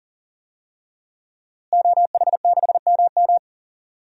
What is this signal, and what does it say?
Morse code, heard as a keyed tone, about 40 wpm, 700 Hz. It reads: OH6MM
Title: DXCC-100 Call Signs - 5 Characters - Repeated 3x - 40wpm